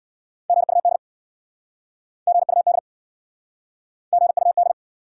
Morse code, heard as a keyed tone, about 69 wpm, 700 Hz. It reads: GUD GUD GUD